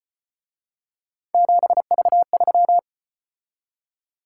Morse code, heard as a keyed tone, about 34 wpm, 700 Hz. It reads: 7V3